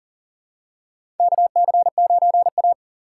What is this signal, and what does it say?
Morse code, heard as a keyed tone, about 40 wpm, 700 Hz. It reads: KC9A
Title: USA Call Signs - 4 Characters - 3x Character Spacing - 40wpm